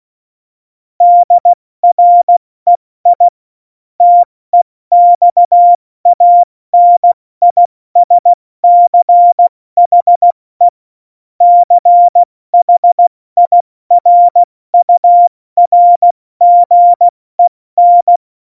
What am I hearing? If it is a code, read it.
Morse code, heard as a keyed tone, about 16 wpm, 700 Hz. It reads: DREI TEXANISCHE CHIRURGEN
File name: Morsekurs / Witze / Witz 01. Dezember 2020, 70 F80